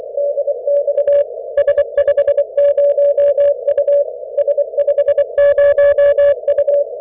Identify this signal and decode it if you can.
Morse code, heard as a keyed tone, about 24 wpm, 570 Hz. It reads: DX S50U S50U
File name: S50U Dan 160M CW at K4SV